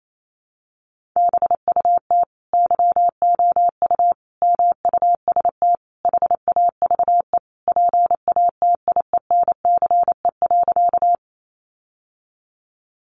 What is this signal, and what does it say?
Morse code, heard as a keyed tone, about 28 wpm, 700 Hz. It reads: BUT YOU MUST HAVE PATIENCE.
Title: Sentences from Top 1300 Words 28wpm